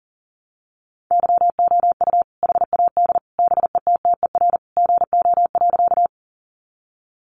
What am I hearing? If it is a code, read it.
Morse code, heard as a keyed tone, about 40 wpm, 700 Hz. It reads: YOU HAD BETTER GO.
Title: Sentences from Top 400 Words 40wpm